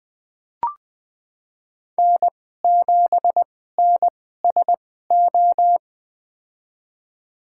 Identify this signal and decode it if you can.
Morse code, heard as a keyed tone, about 20 wpm, spaced as for 14 wpm, 700 Hz. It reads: N7NSO
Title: USA Call Signs - 5 Characters - 2x Character Spacing - 20wpm